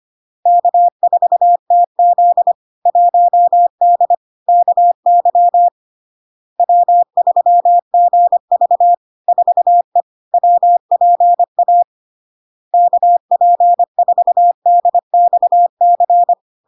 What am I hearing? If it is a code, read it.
Morse code, heard as a keyed tone, about 25 wpm, 700 Hz. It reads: K4TZ 1D KY W3GV 4E WPA KP4DXC
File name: LARK-FD-2022-Log-in-CW-25wpm-700-Hz